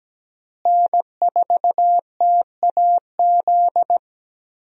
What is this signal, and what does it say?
Morse code, heard as a keyed tone, about 17 wpm, 700 Hz. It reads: N4TAZ
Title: USA Call Signs - 5 Characters - Repeated 2x - 17wpm